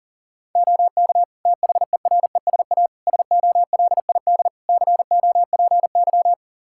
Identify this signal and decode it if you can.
Morse code, heard as a keyed tone, about 37 wpm, 700 Hz. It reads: OK THERESA SOLID COPY
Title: Practice QSO - Exchange 3 - More Info - 40wpm